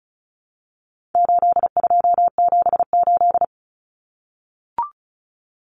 Morse code, heard as a keyed tone, about 34 wpm, 700 Hz. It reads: 8278
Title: Numbers - 4-Digit 35wpm